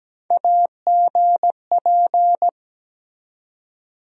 Morse code, heard as a keyed tone, about 17 wpm, 700 Hz. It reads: AGP
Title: USA Call Sign Postfixes - 3 Letters 17wpm